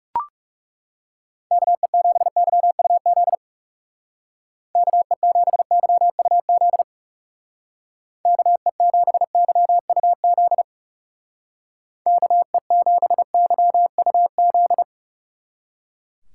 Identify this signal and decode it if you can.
Morse code, heard as a keyed tone, about 36 wpm, 700 Hz. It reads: KE7YUZ KE7YUZ KE7YUZ KE7YUZ
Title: Speed Racing - USA 2x3 Call Signs 30wpm